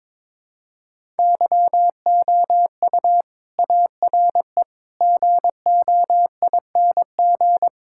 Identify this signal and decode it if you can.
Morse code, heard as a keyed tone, about 22 wpm, 700 Hz. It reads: YOU ARE GOING
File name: practice / 22wpm / Sentences with 4-Word Phrases - 22wpm